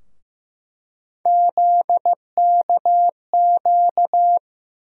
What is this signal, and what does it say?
Morse code, heard as a keyed tone, about 16 wpm, 700 Hz. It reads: ZKQ
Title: USA Call Sign Postfixes - 3 Letters 15wpm